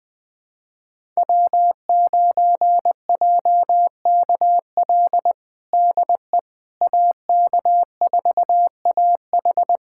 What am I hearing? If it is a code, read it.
Morse code, heard as a keyed tone, about 20 wpm, 700 Hz. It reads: W9JKL DE AK4AH
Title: USA Callsign DE Callsign 20wpm